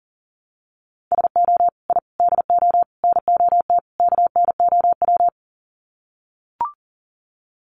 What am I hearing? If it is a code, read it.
Morse code, heard as a keyed tone, about 40 wpm, 700 Hz. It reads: SO I DO NOT KNOW
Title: Sets of 5 Words Encore 40wpm